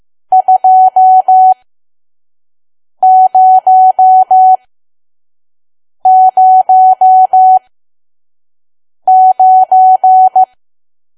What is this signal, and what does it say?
Morse code, heard as a keyed tone, about 15 wpm, spaced as for 5 wpm, 750 Hz. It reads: 2009
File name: Archive / 5 WPM files / PM